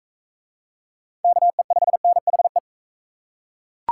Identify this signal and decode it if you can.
Morse code, heard as a keyed tone, about 38 wpm, 700 Hz. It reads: KE5NHE